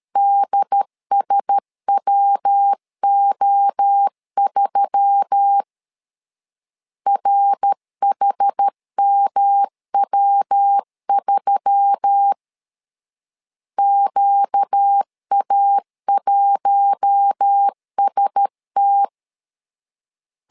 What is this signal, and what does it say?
Morse code, heard as a keyed tone, about 13 wpm, 800 Hz. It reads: DSWO3 RHMW3 QA1ST
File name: LEZIONI / 4 Q1S3 12wpm 2023